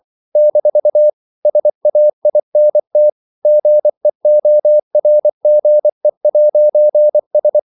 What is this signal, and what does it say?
Morse code, heard as a keyed tone, about 24 wpm, 600 Hz. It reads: - SAINT GEORGE'S
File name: CW / Practice / NationalCapitals24